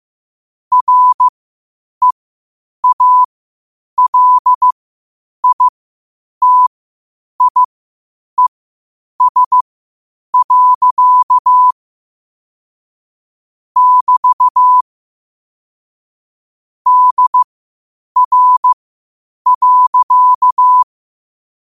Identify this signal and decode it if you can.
Morse code, heard as a keyed tone, about 15 wpm, spaced as for 8 wpm, 1000 Hz. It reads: REALITIES. <BT> DR.
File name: qotdfiles / 8wpm